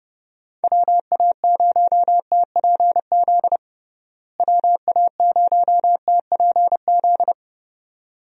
Morse code, heard as a keyed tone, about 30 wpm, 700 Hz. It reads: WA0TPZ WA0TPZ
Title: USA Call Signs - 6 Characters - Repeated 2x - 30wpm